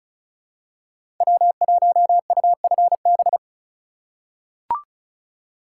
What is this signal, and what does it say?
Morse code, heard as a keyed tone, about 36 wpm, 700 Hz. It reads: W1UFB
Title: USA Call Signs - 5 Characters - 4x Character Spacing - 35wpm